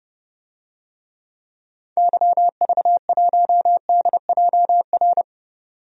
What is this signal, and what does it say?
Morse code, heard as a keyed tone, about 30 wpm, 700 Hz. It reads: YV1DJR
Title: DXCC-100 Call Signs - 6 Characters - 30wpm